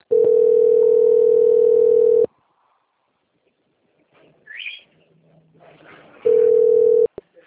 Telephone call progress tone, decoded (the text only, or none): ringback tone